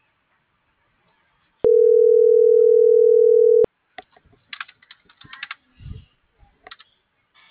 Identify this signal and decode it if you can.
Ringback tone (North America), 440 + 480 Hz.